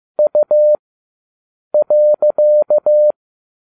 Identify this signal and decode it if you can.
Morse code, heard as a keyed tone, about 15 wpm, 600 Hz. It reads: U .